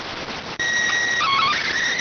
SSTV (slow-scan television), PD 290.